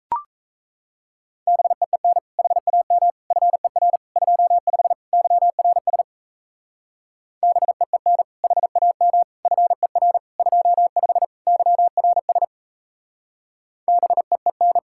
Morse code, heard as a keyed tone, about 34 wpm, 700 Hz. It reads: BEEN HAM FER 25 YRS BEEN HAM FER 25 YRS BEEN
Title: Speed-Racing Common QSO Phrases - 28wpm